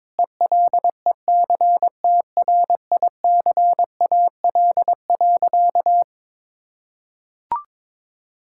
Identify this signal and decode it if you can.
Morse code, heard as a keyed tone, about 22 wpm, 700 Hz. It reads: ELECTRICAL.
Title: Sentences from Top 1000 Words - Review Part 6 - 22wpm